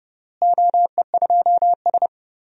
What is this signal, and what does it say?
Morse code, heard as a keyed tone, about 30 wpm, 700 Hz. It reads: OE2S
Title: DXCC-100 Call Signs - 4 Characters - Repeated 3x - 30wpm